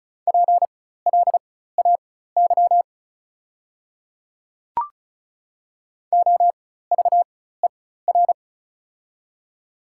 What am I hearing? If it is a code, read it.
Morse code, heard as a keyed tone, about 34 wpm, spaced as for 16 wpm, 700 Hz. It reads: PLAY OVER
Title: 4x Character Spacing - Top 300 Words - 35wpm